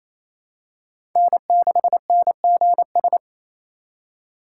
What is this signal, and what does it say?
Morse code, heard as a keyed tone, about 28 wpm, 700 Hz. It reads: N6NGS
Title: USA Call Signs - 5 Characters - 4x Character Spacing - 28wpm